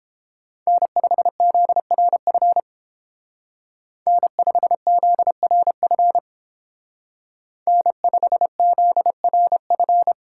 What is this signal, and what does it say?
Morse code, heard as a keyed tone, about 29 wpm, 700 Hz. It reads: N5ZRF N5ZRF N5ZRF